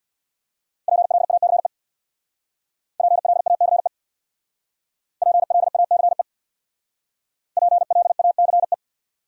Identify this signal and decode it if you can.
Morse code, heard as a keyed tone, about 62 wpm, 700 Hz. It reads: PLACE PLACE PLACE PLACE
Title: Speed-Racing Top 500 Words 50wpm